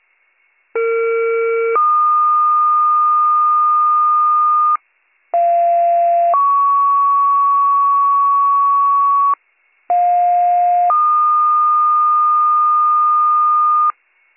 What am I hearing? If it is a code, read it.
Two-tone page, heard as a one-second tone, then a three-second tone, about 460 then 1220 Hz, about 690 then 1120 Hz, about 710 then 1220 Hz. It is heard from 0.7 to 13.9 s.